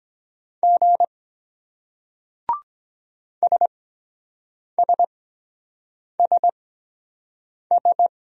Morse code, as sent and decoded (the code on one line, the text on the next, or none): --. / ... ... ... ...
G SSSS